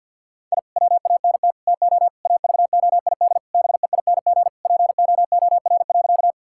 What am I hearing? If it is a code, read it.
Morse code, heard as a keyed tone, about 50 wpm, 700 Hz. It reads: I WANT TO AVOID BEING POOR.
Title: Sentences from Top 1000 Words - Review Part 6 - 50wpm